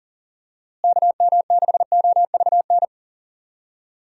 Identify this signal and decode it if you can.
Morse code, heard as a keyed tone, about 37 wpm, 700 Hz. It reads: KM6OVN
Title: USA Call Signs - 6 Characters - Repeated 2x - 40wpm